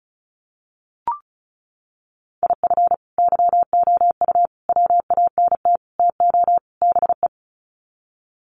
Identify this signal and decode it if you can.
Morse code, heard as a keyed tone, about 34 wpm, 700 Hz. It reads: IF YOU WANT TO BE
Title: Sets of 5 Words Encore 35wpm